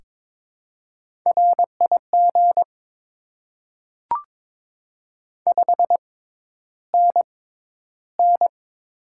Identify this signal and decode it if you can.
Morse code, heard as a keyed tone, about 22 wpm, spaced as for 11 wpm, 700 Hz. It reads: RIG 5NN
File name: Top 20 QSO Elements - 6x Character Spacing - 22wpm